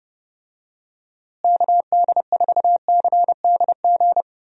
Morse code, heard as a keyed tone, about 32 wpm, 700 Hz. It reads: KD4CDG
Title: USA Call Signs - 6 Characters - Repeated 2x - 30wpm